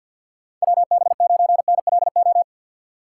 Morse code, heard as a keyed tone, about 48 wpm, 700 Hz. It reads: WB9NLO
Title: USA Call Signs - 4x Character Spacing - 50wpm